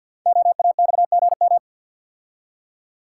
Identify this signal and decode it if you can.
Morse code, heard as a keyed tone, about 50 wpm, 690 Hz. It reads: OAXGM
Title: Code Groups - 5 Letters - 50wpm